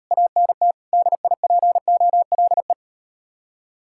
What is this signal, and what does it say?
Morse code, heard as a keyed tone, about 39 wpm, 700 Hz. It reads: ANT DIPOLE